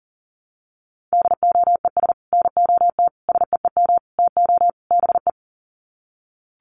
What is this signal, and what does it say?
Morse code, heard as a keyed tone, about 40 wpm, 700 Hz. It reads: DOES NOT SEEM TO BE
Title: Sets of 5 Words 40wpm